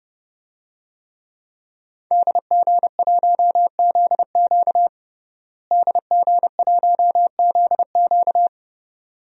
Morse code, heard as a keyed tone, about 30 wpm, 700 Hz. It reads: DG1ZQ DG1ZQ
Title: DXCC-100 Call Signs - 5 Characters - Repeated 2x - 30wpm